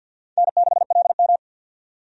Morse code, heard as a keyed tone, about 53 wpm, 700 Hz. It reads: N6LM
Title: USA Call Signs - 4 Characters - 50wpm